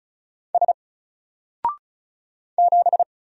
Morse code, heard as a keyed tone, about 35 wpm, 700 Hz. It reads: S 7
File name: Single-Letter-Number-35wpm